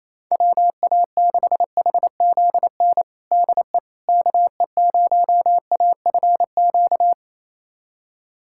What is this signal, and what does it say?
Morse code, heard as a keyed tone, about 27 wpm, 700 Hz. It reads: WA6HZN DE KE0AFQ
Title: USA Callsign DE Callsign - 6 Characters - 28wpm